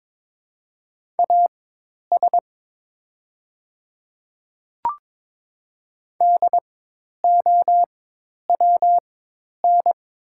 Morse code, heard as a keyed tone, about 22 wpm, spaced as for 10 wpm, 700 Hz. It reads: AS DOWN
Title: 4x Character Spacing - Top 100 Words - 22wpm